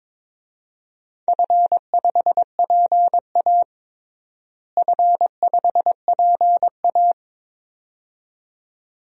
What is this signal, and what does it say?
Morse code, heard as a keyed tone, about 22 wpm, 700 Hz. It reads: F5PA F5PA
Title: DXCC-100 Call Signs - 4 Characters - Repeated 2x - 22wpm